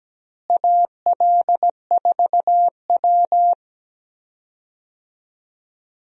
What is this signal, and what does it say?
Morse code, heard as a keyed tone, about 17 wpm, 700 Hz. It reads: AL4W
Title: USA Call Signs - 4 Characters - Repeated 2x - 17wpm